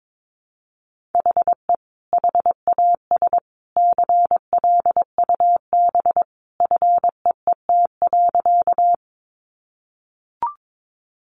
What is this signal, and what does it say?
Morse code, heard as a keyed tone, about 22 wpm, 700 Hz. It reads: HE HAS CLUB FEET.
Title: Sentences from Top 1200 Words 22wpm